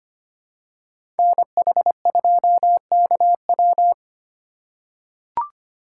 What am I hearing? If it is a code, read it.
Morse code, heard as a keyed tone, about 25 wpm, 700 Hz. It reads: NH2KW
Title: USA Call Signs - 5 Characters - 6x Character Spacing - 25wpm